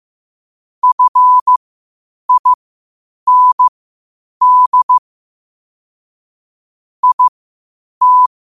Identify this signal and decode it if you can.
Morse code, heard as a keyed tone, about 15 wpm, spaced as for 8 wpm, 1000 Hz. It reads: FIND IT
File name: qotdfiles / 8wpm